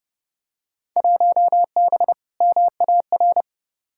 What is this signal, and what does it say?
Morse code, heard as a keyed tone, about 32 wpm, 700 Hz. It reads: 1B MAR
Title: Field Day Exchanges Encore - 30wpm